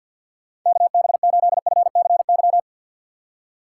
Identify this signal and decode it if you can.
Morse code, heard as a keyed tone, about 53 wpm, 700 Hz. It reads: KB8FCY
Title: USA Call Signs - 6 Characters - 6x Character Spacing - 50wpm